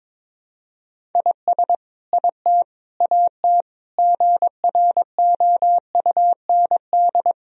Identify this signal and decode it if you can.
Morse code, heard as a keyed tone, about 22 wpm, 700 Hz. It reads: IS IT AT GROUND
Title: Sentences from Top 1000 Words - Review Part 4 - 22wpm